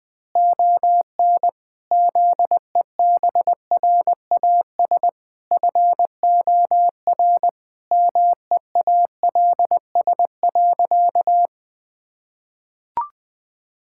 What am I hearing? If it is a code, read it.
Morse code, heard as a keyed tone, about 20 wpm, 700 Hz. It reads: ON ZEBRAS FOR MEALS.
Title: Pangram Challenge v2 - 20wpm